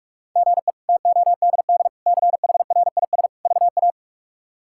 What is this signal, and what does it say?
Morse code, heard as a keyed tone, about 45 wpm, 700 Hz. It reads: GE TODD CHRIS VA
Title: K1USN SST QSO Practice - 45wpm